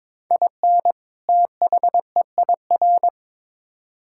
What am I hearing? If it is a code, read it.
Morse code, heard as a keyed tone, about 22 wpm, 700 Hz. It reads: IN THEIR
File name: Build A Phrase - 2-Word Phrases - 22wpm